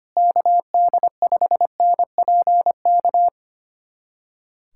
Morse code, heard as a keyed tone, about 25 wpm, 700 Hz. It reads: KD5NPK
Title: USA Callsign DE Callsign - 6 Characters - 25wpm